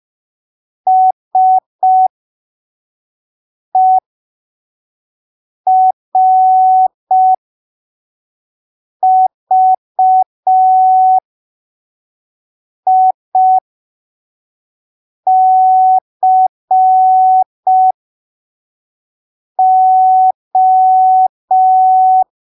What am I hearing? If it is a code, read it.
Morse code, heard as a keyed tone, about 5 wpm, spaced as for 3 wpm, 750 Hz. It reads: SERVICO
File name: arquivos / texto / texto 01 5ppm